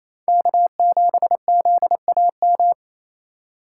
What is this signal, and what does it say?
Morse code, heard as a keyed tone, about 28 wpm, 700 Hz. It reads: K7ZAM
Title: USA Call Signs - 5 Characters - 2x Character Spacing - 28wpm